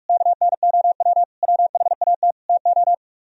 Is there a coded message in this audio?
Yes: Morse code from 0.1 to 2.9 s, heard as a keyed tone, about 45 wpm, 700 Hz. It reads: KNOW WHAT TO